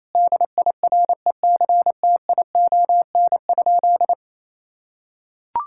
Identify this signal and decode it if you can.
Morse code, heard as a keyed tone, about 28 wpm, 700 Hz. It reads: DIRECTION?
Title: Sentences from Top 1000 Words - Review Part 5 - 28wpm